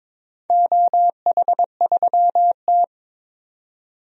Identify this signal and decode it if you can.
Morse code, heard as a keyed tone, about 22 wpm, 700 Hz. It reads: OH3T
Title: DXCC-100 Call Signs - 4 Characters - 22wpm